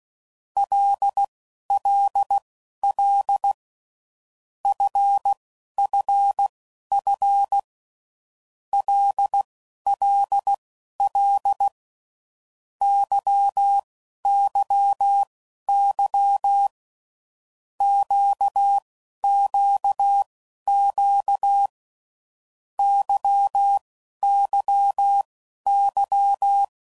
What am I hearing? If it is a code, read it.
Morse code, heard as a keyed tone, about 16 wpm, spaced as for 11 wpm, 800 Hz. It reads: LLL FFF LLL YYY QQQ YYY
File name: Groups / Basic Letter Groups / LFCYQ